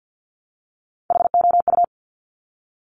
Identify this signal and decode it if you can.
Morse code, heard as a keyed tone, about 50 wpm, 700 Hz. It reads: HOU
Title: Top Three Letter Combinations in Words 50wpm